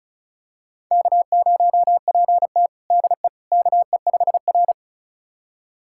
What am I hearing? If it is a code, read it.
Morse code, heard as a keyed tone, about 34 wpm, 700 Hz. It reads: K0PT DE KE5R
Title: USA Callsign DE Callsign - 4 Characters - 35wpm